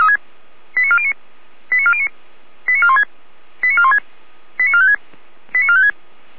Five-tone selective call (ZVEI), heard as quick runs of five tones, tones of about 70 ms. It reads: 89409 89409 89416 89416 89456 89456